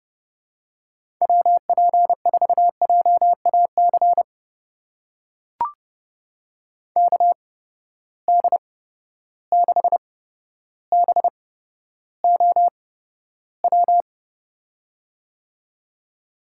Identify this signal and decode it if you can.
Morse code, heard as a keyed tone, about 31 wpm, spaced as for 13 wpm, 700 Hz. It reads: WP4JAC KD6BOW
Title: USA Call Signs - 6 Characters - 8x Character Spacing - 30wpm